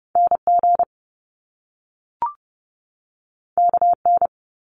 Morse code, heard as a keyed tone, about 30 wpm, 700 Hz. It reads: NG KN
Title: Top Two Letter Combinations in Words 30wpm